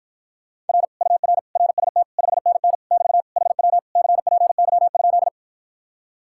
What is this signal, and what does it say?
Morse code, heard as a keyed tone, about 54 wpm, 700 Hz. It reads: R UR RST 5NN <BT> HW CPY?